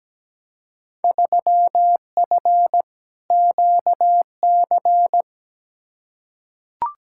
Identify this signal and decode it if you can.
Morse code, heard as a keyed tone, about 17 wpm, 700 Hz. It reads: 3F QC
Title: Field Day Exchanges - 17wpm